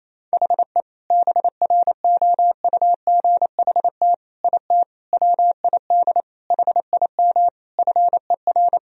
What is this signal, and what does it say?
Morse code, heard as a keyed tone, about 28 wpm, 700 Hz. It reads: HE BROUGHT IT WID HIM FER
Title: Sentences from Top 500 Words and Abbreviations 28wpm